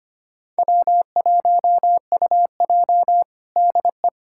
Morse code, heard as a keyed tone, about 25 wpm, 700 Hz. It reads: W1UJ DE
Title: Practice QSO - Exchange 4 - Ending - 25wpm